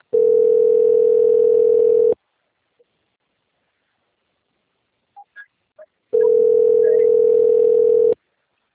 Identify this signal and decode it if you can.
Ringback tone (North America), 440 + 480 Hz.
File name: out- - - - - .25416